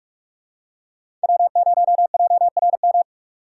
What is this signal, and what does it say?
Morse code, heard as a keyed tone, about 41 wpm, 700 Hz. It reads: W0JRM
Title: USA Call Signs - 5 Characters - 6x Character Spacing - 45wpm